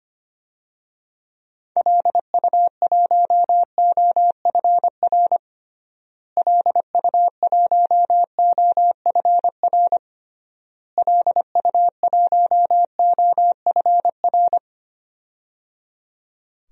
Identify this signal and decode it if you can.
Morse code, heard as a keyed tone, about 25 wpm, 700 Hz. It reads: LU1OFR LU1OFR LU1OFR
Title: DXCC-100 Call Signs - 6 Characters - Repeated 3x - 25wpm